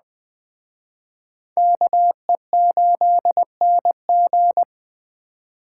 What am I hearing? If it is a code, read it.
Morse code, heard as a keyed tone, about 21 wpm, 700 Hz. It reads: KE8NG